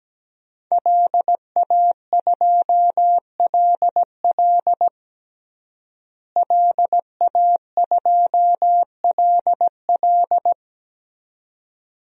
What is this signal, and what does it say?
Morse code, heard as a keyed tone, about 17 wpm, 700 Hz. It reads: LA2LL LA2LL
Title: DXCC-100 Call Signs - 5 Characters - Repeated 2x - 17wpm